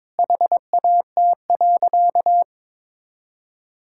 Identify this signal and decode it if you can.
Morse code, heard as a keyed tone, about 22 wpm, 700 Hz. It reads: HAT.
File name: practice / 22wpm / Easy Sentences with Hints - 22wpm